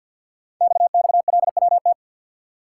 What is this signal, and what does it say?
Morse code, heard as a keyed tone, about 50 wpm, 700 Hz. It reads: XXLWT
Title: Code Groups - 5 Letters - 50wpm